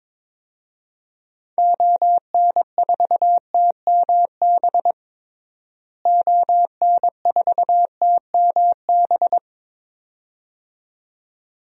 Morse code, heard as a keyed tone, about 22 wpm, 700 Hz. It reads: ON4TMB ON4TMB
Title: DXCC-100 Call Signs - 6 Characters - Repeated 2x - 22wpm